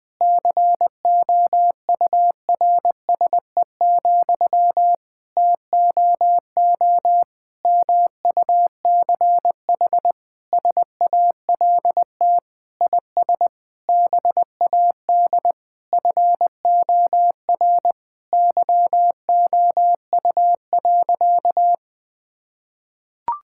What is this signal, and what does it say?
Morse code, heard as a keyed tone, about 21 wpm, 700 Hz. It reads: COURSE, TOO MUCH SALT IS BAD FOR YOU.
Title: Sentences from Top 1000 Words - Review Part 5 - 20wpm